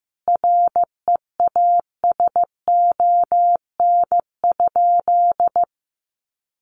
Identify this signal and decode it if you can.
Morse code, heard as a keyed tone, about 15 wpm, 700 Hz. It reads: REASON?
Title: Sentences from Top 700 Words 15wpm